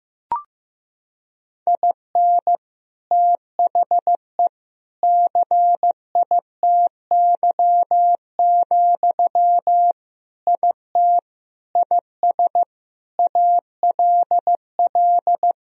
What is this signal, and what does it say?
Morse code, heard as a keyed tone, about 15 wpm, 700 Hz. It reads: IN THE CITY, IT IS ALL
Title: Sentences from Top 1000 Words - Review Part 3 - 15wpm